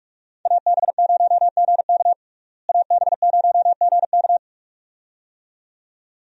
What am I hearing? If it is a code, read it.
Morse code, heard as a keyed tone, about 41 wpm, 700 Hz. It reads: AB0GK AB0GK
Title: USA Call Signs - 5 Characters - Repeated 2x - 45wpm